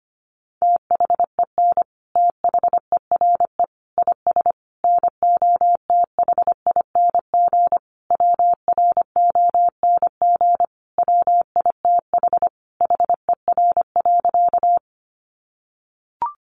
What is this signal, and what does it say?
Morse code, heard as a keyed tone, about 25 wpm, 700 Hz. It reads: THEN THERE IS NOTHING WRONG WITH HER.
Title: Sentences from Top 900 Words 25wpm